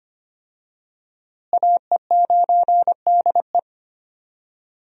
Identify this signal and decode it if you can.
Morse code, heard as a keyed tone, about 25 wpm, 700 Hz. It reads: AE9DE
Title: USA Call Signs - 5 Characters - Repeated 2x - 25wpm